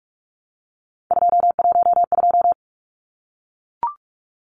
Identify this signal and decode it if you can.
Morse code, heard as a keyed tone, about 44 wpm, 700 Hz. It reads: 212
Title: Numbers - 3-Digit 45wpm